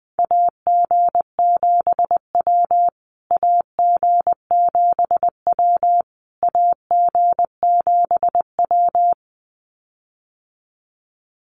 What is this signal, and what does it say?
Morse code, heard as a keyed tone, about 20 wpm, 700 Hz. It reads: AG7W AG7W AG7W
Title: USA Call Signs Repeated 3x 20wpm